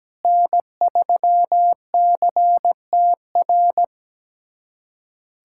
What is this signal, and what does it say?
Morse code, heard as a keyed tone, about 17 wpm, 700 Hz. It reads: N3CTR
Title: USA Callsign DE Callsign 17wpm